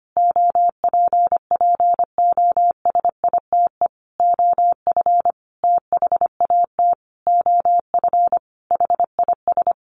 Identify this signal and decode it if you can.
Morse code, heard as a keyed tone, about 25 wpm, 700 Hz. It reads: OPPOSITE OF THAT OF HIS
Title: Sentences from Top 1300 Words 25wpm